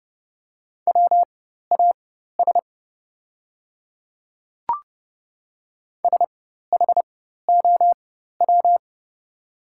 Morse code, heard as a keyed tone, about 30 wpm, spaced as for 14 wpm, 700 Hz. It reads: WAS SHOW
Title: 4x Character Spacing - Top 200 Words - 30wpm